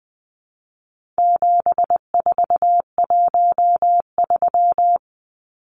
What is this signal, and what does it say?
Morse code, heard as a keyed tone, about 20 wpm, 700 Hz. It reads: 7413
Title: Numbers - 4-Digit 20wpm